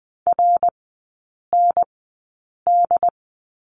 Morse code, heard as a keyed tone, about 20 wpm, spaced as for 8 wpm, 700 Hz. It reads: RND
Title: 067 Lesson 12e - Letter d - Three Characters 20wpm